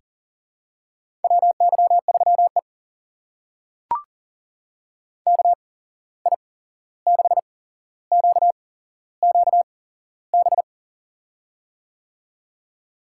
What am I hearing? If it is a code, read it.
Morse code, heard as a keyed tone, about 40 wpm, spaced as for 15 wpm, 700 Hz. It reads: WY3E KI6QQB
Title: USA Call Signs - 8x Character Spacing - 40wpm